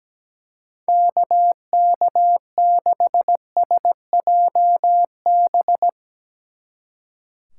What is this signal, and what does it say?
Morse code, heard as a keyed tone, about 18 wpm, 700 Hz. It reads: KK6SJB
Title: Speed Racing - USA 2x3 Call Signs 17wpm